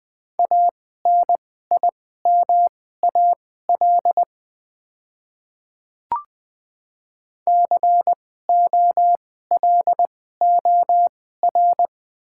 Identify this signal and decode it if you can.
Morse code, heard as a keyed tone, about 20 wpm, spaced as for 15 wpm, 700 Hz. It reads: ANIMAL COLOR
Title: 2x Character Spacing - Top 300 Words - 20wpm